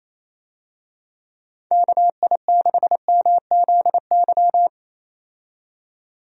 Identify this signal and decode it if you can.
Morse code, heard as a keyed tone, about 28 wpm, 700 Hz. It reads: KI6MZY
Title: USA Call Signs - 6 Characters - 28wpm